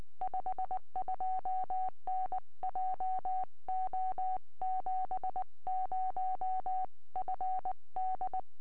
Morse code, heard as a keyed tone, about 19 wpm, 750 Hz. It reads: 52N JO70FD
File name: tropo / g3wzt 2m